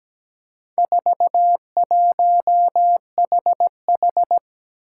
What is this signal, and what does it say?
Morse code, heard as a keyed tone, about 17 wpm, 700 Hz. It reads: 41HH
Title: Four Challenging Letter-Number Combinations 17wpm